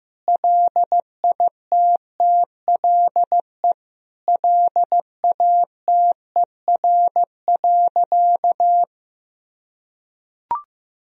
Morse code, heard as a keyed tone, about 14 wpm, 700 Hz. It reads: LITTLE LATER.
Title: Sentences from Top 1000 Words - Review Part 3 - 15wpm